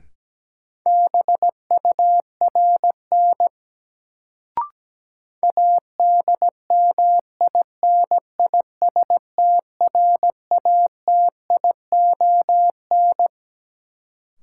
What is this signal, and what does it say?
Morse code, heard as a keyed tone, about 17 wpm, 700 Hz. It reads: BURN ADMINISTRATION
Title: Sentences from Top 1300 Words - New Word Drill Alt - 17wpm